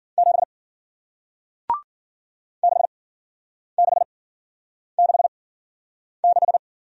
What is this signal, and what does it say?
Morse code, heard as a keyed tone, about 55 wpm, spaced as for 9 wpm, 700 Hz. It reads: 7 6666